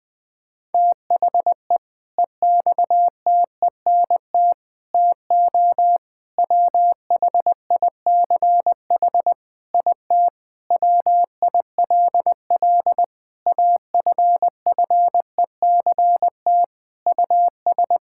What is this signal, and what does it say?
Morse code, heard as a keyed tone, about 20 wpm, 700 Hz. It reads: THE EXTENT TO WHICH IT WILL AFFECT US
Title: Sentences with 4-Word Phrases - 20wpm